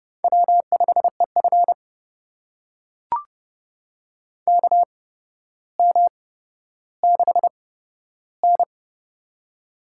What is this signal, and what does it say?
Morse code, heard as a keyed tone, about 30 wpm, spaced as for 13 wpm, 700 Hz. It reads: W5EF KM6N
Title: USA Call Signs - 4 Characters - 8x Character Spacing - 30wpm